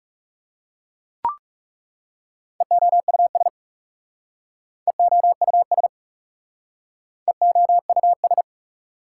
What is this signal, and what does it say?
Morse code, heard as a keyed tone, about 40 wpm, 690 Hz. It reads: EOUS EOUS EOUS